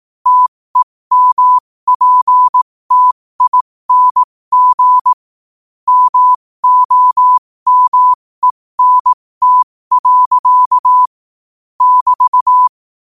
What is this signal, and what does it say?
Morse code, heard as a keyed tone, about 18 wpm, spaced as for 15 wpm, 1000 Hz. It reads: TEMPTING MOMENT. <BT>
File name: qotdfiles / 15wpm